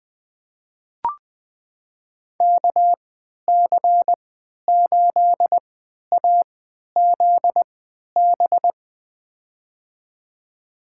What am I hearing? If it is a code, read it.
Morse code, heard as a keyed tone, about 20 wpm, spaced as for 11 wpm, 700 Hz. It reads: KC8AZB
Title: USA Call Signs - 6 Characters - 3x Character Spacing - 20wpm